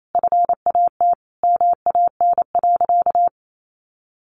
Morse code, heard as a keyed tone, about 28 wpm, 700 Hz. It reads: FAT MAN.
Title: Sentences from Top 1100 Words 28wpm